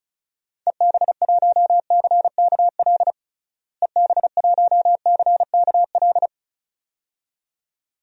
Morse code, heard as a keyed tone, about 36 wpm, 700 Hz. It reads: EB1CKL EB1CKL
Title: DXCC-100 Call Signs - 6 Characters - Repeated 2x - 35wpm